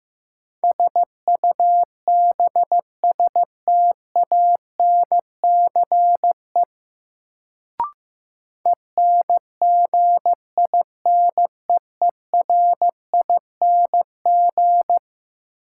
Morse code, heard as a keyed tone, about 15 wpm, 700 Hz. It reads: SUBSTANCE ENGINEERING
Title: Sentences from Top 1500 Words - New Word Drill - 15wpm